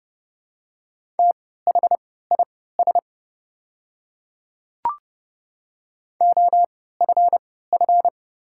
Morse code, heard as a keyed tone, about 30 wpm, spaced as for 17 wpm, 700 Hz. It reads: THIS OFF
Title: 3x Character Spacing - Top 200 Words - 30wpm